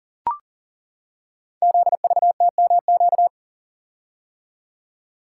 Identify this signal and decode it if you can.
Morse code, heard as a keyed tone, about 42 wpm, 690 Hz. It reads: ZVTMQ
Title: Code Groups - 5 Letters - 40wpm